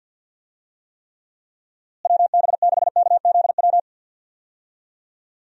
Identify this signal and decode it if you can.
Morse code, heard as a keyed tone, about 47 wpm, 700 Hz. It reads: WB6KZW
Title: USA Call Signs - 6 Characters - Repeated 3x - 50wpm